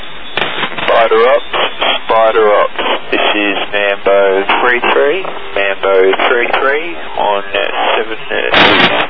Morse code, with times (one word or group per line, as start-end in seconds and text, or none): none